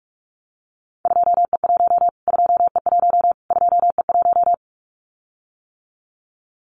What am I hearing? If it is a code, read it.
Morse code, heard as a keyed tone, about 44 wpm, 700 Hz. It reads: 2E1 2E1 2E1
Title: Callsign Prefixes for Top 25 DXCC - Repeated 3x - 45wpm